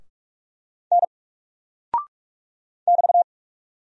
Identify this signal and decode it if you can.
Morse code, heard as a keyed tone, about 44 wpm, 700 Hz. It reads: N <BT>
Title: Single Character (Letter:Number:Punctuation:Prosign) 45wpm